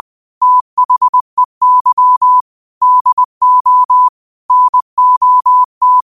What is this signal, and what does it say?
Morse code, heard as a keyed tone, about 20 wpm, 1000 Hz. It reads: THEY DO NOT
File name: qotdfiles / 20wpm